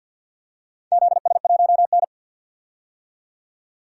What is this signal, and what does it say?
Morse code, heard as a keyed tone, about 52 wpm, 700 Hz. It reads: ZS1N
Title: DXCC-100 Call Signs - 4 Characters - 50wpm